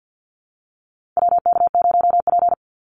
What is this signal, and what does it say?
Morse code, heard as a keyed tone, about 50 wpm, 700 Hz. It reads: WK0P
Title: Call Signs - 50wpm